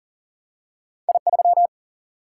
Morse code, heard as a keyed tone, about 38 wpm, 700 Hz. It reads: I3
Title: Single-Letter Single-Number 40wpm